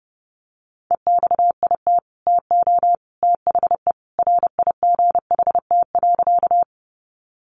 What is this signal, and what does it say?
Morse code, heard as a keyed tone, about 30 wpm, 700 Hz. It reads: EXIT TO THE RIGHT.
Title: Sentences from Top 1100 Words 30wpm